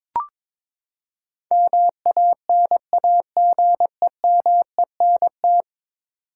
Morse code, heard as a keyed tone, about 22 wpm, 700 Hz. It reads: MANAGEMENT
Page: morse-code-ninja-practice-22wpm